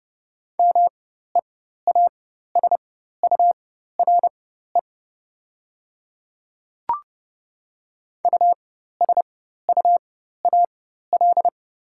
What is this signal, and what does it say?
Morse code, heard as a keyed tone, about 30 wpm, spaced as for 14 wpm, 700 Hz. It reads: MEASURE USUAL
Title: 4x Character Spacing - Top 500 Words - 30wpm